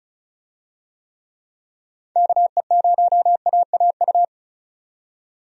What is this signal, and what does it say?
Morse code, heard as a keyed tone, about 34 wpm, 700 Hz. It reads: KE0AAU
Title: USA Call Signs - 6 Characters - 35wpm